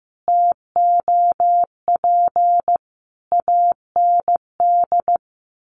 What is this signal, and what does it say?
Morse code, heard as a keyed tone, about 15 wpm, 700 Hz. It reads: TOP AND
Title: Sentences from Top 1400 Words 15wpm